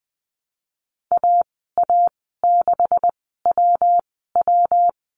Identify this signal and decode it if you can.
Morse code, heard as a keyed tone, about 20 wpm, spaced as for 15 wpm, 700 Hz. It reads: AA6WW
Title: 147 Lesson 24g - Letter y - Call Signs 20wpm